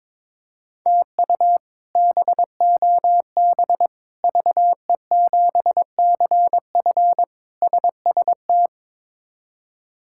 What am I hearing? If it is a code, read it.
Morse code, heard as a keyed tone, about 22 wpm, 700 Hz. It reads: TU BOB VE7CF SST
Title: K1USN SST QSO Practice - 22wpm